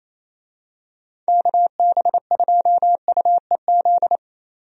Morse code, heard as a keyed tone, about 28 wpm, 700 Hz. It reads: KB2UEZ
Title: USA Call Signs - 6 Characters - 28wpm